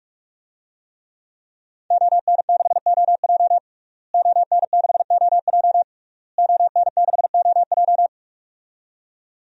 Morse code, heard as a keyed tone, about 44 wpm, 700 Hz. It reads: ON6OJ ON6OJ ON6OJ
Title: DXCC-100 Call Signs - 5 Characters - Repeated 3x - 45wpm